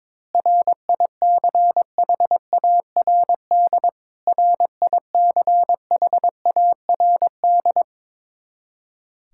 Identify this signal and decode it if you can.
Morse code, heard as a keyed tone, about 22 wpm, 700 Hz. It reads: RICHARD RICHARD
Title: Practice QSO - Exchange 1 - Beginning - 22wpm